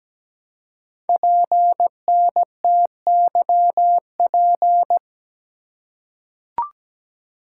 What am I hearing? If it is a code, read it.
Morse code, heard as a keyed tone, about 17 wpm, 700 Hz. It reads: PNTYP
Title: Code Groups - 5 Letter:Numbers - 17wpm